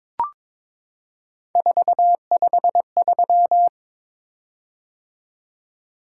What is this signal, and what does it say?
Morse code, heard as a keyed tone, about 22 wpm, 700 Hz. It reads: 453
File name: practice / 22wpm / Three Letter-Number Combinations - 22wpm